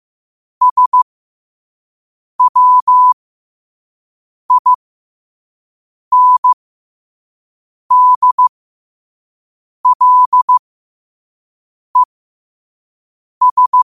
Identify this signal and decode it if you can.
Morse code, heard as a keyed tone, about 15 wpm, spaced as for 5 wpm, 1000 Hz. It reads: SWINDLES
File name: qotdfiles / 5wpm